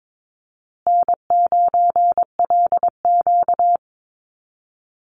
Morse code, heard as a keyed tone, about 22 wpm, 700 Hz. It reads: N9LQ
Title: Easy Call Signs Repeated 2x - 22wpm